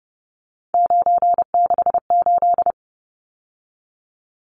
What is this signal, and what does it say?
Morse code, heard as a keyed tone, about 30 wpm, 700 Hz. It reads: 968
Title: Numbers - 3-Digit 30wpm